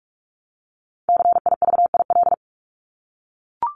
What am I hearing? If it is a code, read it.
Morse code, heard as a keyed tone, about 44 wpm, 700 Hz. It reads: CIVIL
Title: Top 1751-2000 Words 45wpm